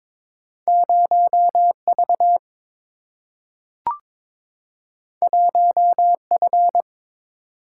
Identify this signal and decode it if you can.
Morse code, heard as a keyed tone, about 22 wpm, 700 Hz. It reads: 0V 1F
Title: Single-Letter Single-Number 22wpm